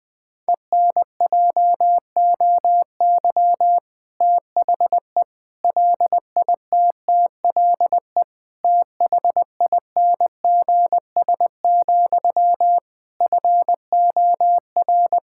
Morse code, heard as a keyed tone, about 20 wpm, 700 Hz. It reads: ENJOY THE LITTLE THINGS, FOR
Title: Difficult Sentences with Hints - 20wpm